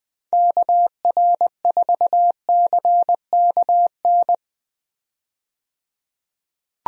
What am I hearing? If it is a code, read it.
Morse code, heard as a keyed tone, about 20 wpm, 700 Hz. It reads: KR4CKN
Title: USA Call Signs - 6 Characters - 20wpm